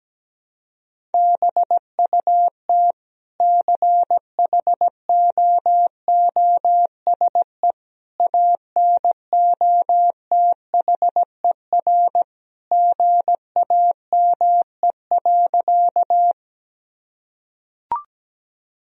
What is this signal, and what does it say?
Morse code, heard as a keyed tone, about 17 wpm, 700 Hz. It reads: BUT CHOOSE ANOTHER GAME.
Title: Sentences from Top 1000 Words - Review Part 4 - 17wpm